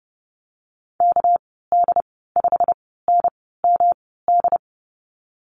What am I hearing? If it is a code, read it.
Morse code, heard as a keyed tone, about 30 wpm, spaced as for 17 wpm, 700 Hz. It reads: KD5NMD
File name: Lesson 37g - Letter x - Call Signs 30wpm